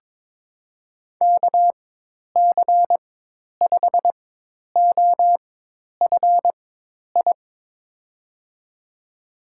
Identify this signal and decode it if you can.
Morse code, heard as a keyed tone, about 22 wpm, spaced as for 10 wpm, 700 Hz. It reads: KC5OFI
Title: USA Call Signs - 6 Characters - 4x Character Spacing - 22wpm